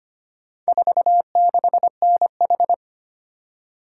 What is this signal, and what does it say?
Morse code, heard as a keyed tone, about 25 wpm, 700 Hz. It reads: 46NH
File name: practice / 25wpm / Four Letter-Number Combinations - 25wpm